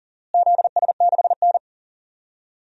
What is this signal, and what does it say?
Morse code, heard as a keyed tone, about 40 wpm, 700 Hz. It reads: ZS6N
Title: DXCC-100 Call Signs - 4 Characters - Repeated 3x - 40wpm